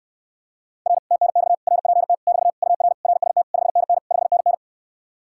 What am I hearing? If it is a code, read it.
Morse code, heard as a keyed tone, about 69 wpm, 700 Hz. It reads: R TNX RPT <BT> UR RST 5NN 5NN